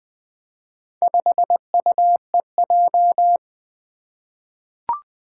Morse code, heard as a keyed tone, about 20 wpm, 700 Hz. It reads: 5UEJ
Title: Four Challenging Letter-Number Combinations 20wpm